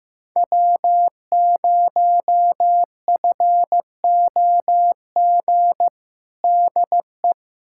Morse code, heard as a keyed tone, about 15 wpm, 700 Hz. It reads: W0FOG DE
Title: USA Callsign DE Callsign - 5 Characters - 15wpm